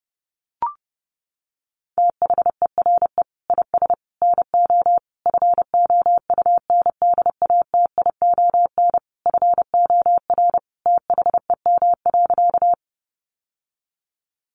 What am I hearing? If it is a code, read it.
Morse code, heard as a keyed tone, about 30 wpm, 700 Hz. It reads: THERE IS NO FOUNDATION FOR THEM.
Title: Sentences from Top 1200 Words 30wpm